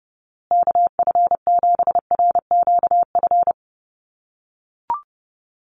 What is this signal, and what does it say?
Morse code, heard as a keyed tone, about 30 wpm, 700 Hz. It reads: KF7RQF
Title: USA Call Signs Repeated 2x 30wpm